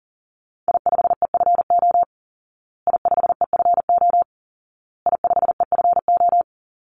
Morse code, heard as a keyed tone, about 40 wpm, 700 Hz. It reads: I5EFO I5EFO I5EFO
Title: Call Signs Repeated 2x - 40wpm